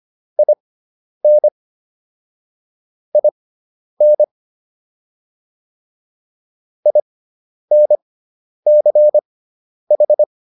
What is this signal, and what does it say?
Morse code, heard as a keyed tone, about 25 wpm, spaced as for 10 wpm, 600 Hz. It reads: IN IN INCH